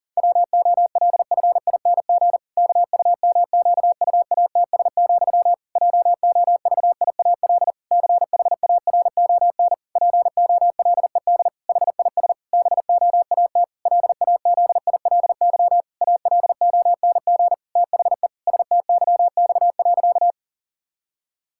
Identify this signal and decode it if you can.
Morse code, heard as a keyed tone, about 40 wpm, 700 Hz. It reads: WOLFING KUMQUATS, JOVIAL CHARON POLED HIS BOAT LAZILY ALONG THE STYX.
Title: Pangram Challenge v2 - 40wpm